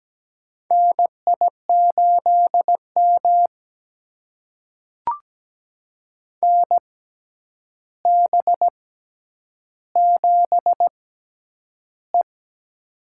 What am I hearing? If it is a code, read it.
Morse code, heard as a keyed tone, about 17 wpm, spaced as for 9 wpm, 700 Hz. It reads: NI8M NB7E
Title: USA Call Signs - 4 Characters - 6x Character Spacing - 17wpm